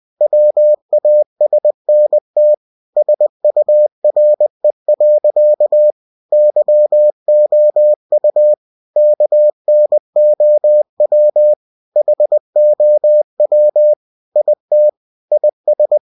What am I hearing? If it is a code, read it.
Morse code, heard as a keyed tone, about 20 wpm, 600 Hz. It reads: WASNT SURE. YOU KNOW HOW IT IS
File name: cw-books / winnie-the-pooh / winnie-20wpm-page-0015